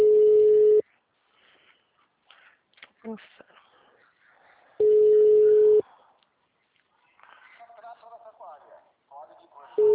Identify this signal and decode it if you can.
Ringback tone (Europe, 425 Hz), with speech.